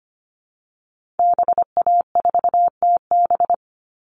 Morse code, heard as a keyed tone, about 25 wpm, 700 Hz. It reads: BA4TB
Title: Call Signs Repeated 3x - 25wpm